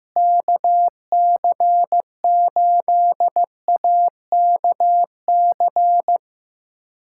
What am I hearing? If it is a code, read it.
Morse code, heard as a keyed tone, about 16 wpm, 700 Hz. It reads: KC8AKC
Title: USA Callsign DE Callsign - 6 Characters - 15wpm